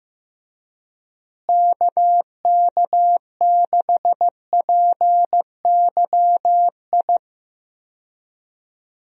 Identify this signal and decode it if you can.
Morse code, heard as a keyed tone, about 15 wpm, 700 Hz. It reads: KK6PYI
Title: USA Call Signs - 6 Characters - Repeated 3x - 15wpm